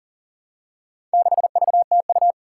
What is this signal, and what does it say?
Morse code, heard as a keyed tone, about 40 wpm, 700 Hz. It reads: 6VTU
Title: Four Challenging Letter-Number Combinations 40wpm